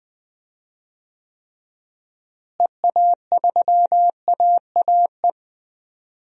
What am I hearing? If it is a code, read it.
Morse code, heard as a keyed tone, about 21 wpm, 700 Hz. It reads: EA3AAE
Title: DXCC-100 Call Signs - 6 Characters - Repeated 3x - 20wpm